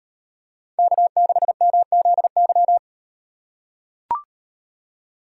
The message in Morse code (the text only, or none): K6MZY